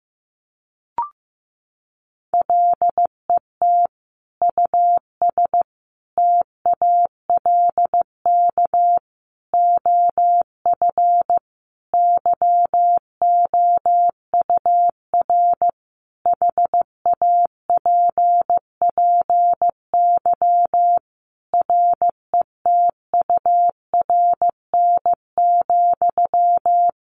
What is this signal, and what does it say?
Morse code, heard as a keyed tone, about 15 wpm, 700 Hz. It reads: LET US TALK OF YOUR HAPPY RETURN,
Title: Sentences from Top 1000 Words 15wpm